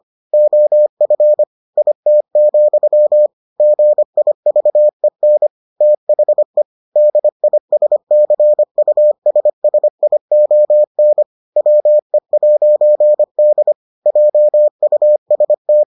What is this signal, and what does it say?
Morse code, heard as a keyed tone, about 25 wpm, 600 Hz. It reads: OF IT, GIVEN THE DISCUSSION WE'D JUST